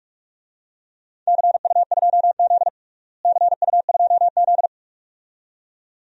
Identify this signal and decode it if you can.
Morse code, heard as a keyed tone, about 42 wpm, 700 Hz. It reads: CU2Z CU2Z
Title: DXCC-100 Call Signs - 4 Characters - Repeated 2x - 45wpm